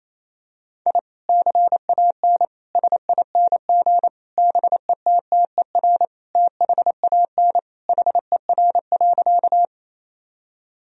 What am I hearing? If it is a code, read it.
Morse code, heard as a keyed tone, about 28 wpm, 700 Hz. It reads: I CAN SING BETTER THAN HER.
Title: Sentences from Top 1000 Words - Review Part 2 - 28wpm